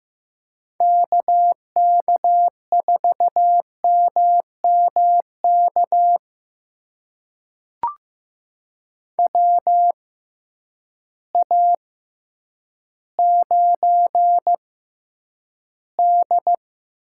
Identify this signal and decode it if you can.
Morse code, heard as a keyed tone, about 15 wpm, spaced as for 9 wpm, 700 Hz. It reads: KK4MMK WA9D
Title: USA Call Signs - 6x Character Spacing - 15wpm